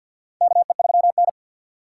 Morse code, heard as a keyed tone, about 47 wpm, 700 Hz. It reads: KE3N
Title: USA Call Signs - 4 Characters - 2x Character Spacing - 50wpm